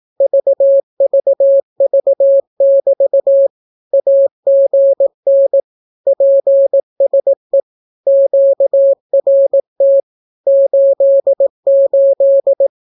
Morse code, heard as a keyed tone, about 18 wpm, 550 Hz. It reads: VVV<BT> AGN PSE QRT 88
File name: naukapodstaw / skroty / skroty